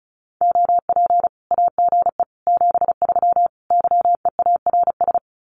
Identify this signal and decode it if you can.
Morse code, heard as a keyed tone, about 34 wpm, 700 Hz. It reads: OP AGE 73 YEARS
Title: Common QSO Phrases 35wpm